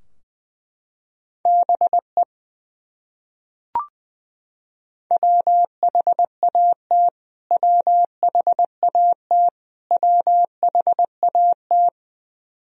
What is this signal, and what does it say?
Morse code, heard as a keyed tone, about 20 wpm, 700 Hz. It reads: BE WHAT WHAT WHAT
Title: Top 100 Words Repeated 20wpm